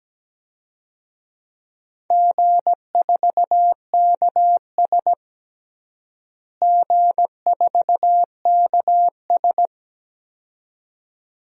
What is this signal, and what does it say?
Morse code, heard as a keyed tone, about 17 wpm, 700 Hz. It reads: G4KS G4KS
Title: DXCC-100 Call Signs - 4 Characters - Repeated 2x - 17wpm